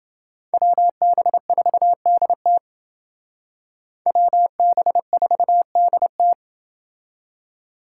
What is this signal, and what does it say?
Morse code, heard as a keyed tone, about 29 wpm, 700 Hz. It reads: WB4DT WB4DT